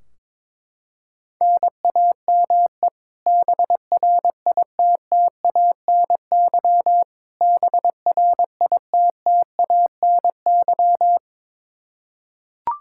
Morse code, heard as a keyed tone, about 22 wpm, 700 Hz. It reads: NAME BRITTANY BRITTANY
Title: Practice QSO - Exchange 1 - Beginning - 22wpm